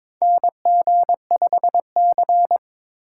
Morse code, heard as a keyed tone, about 22 wpm, 700 Hz. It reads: NG5C
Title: USA Call Signs - 4 Characters - 2x Character Spacing - 22wpm